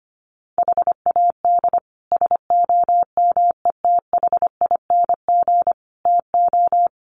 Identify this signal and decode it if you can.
Morse code, heard as a keyed tone, about 25 wpm, 700 Hz. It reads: HAD SOMETHING TO
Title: Sets of 5 Words 25wpm